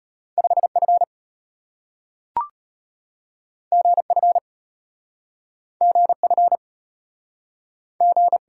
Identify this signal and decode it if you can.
Morse code, heard as a keyed tone, about 33 wpm, 700 Hz. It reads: 5F GF GF G